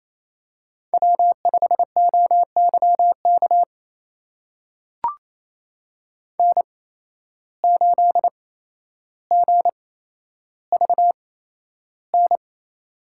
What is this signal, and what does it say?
Morse code, heard as a keyed tone, about 27 wpm, spaced as for 12 wpm, 700 Hz. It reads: W5OYK N8GVN
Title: USA Call Signs - 5 Characters - 8x Character Spacing - 28wpm